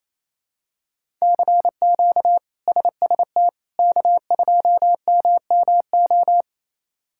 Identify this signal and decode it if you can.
Morse code, heard as a keyed tone, about 27 wpm, 700 Hz. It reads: CQ SST K2MMO
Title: K1USN SST QSO Practice - 28wpm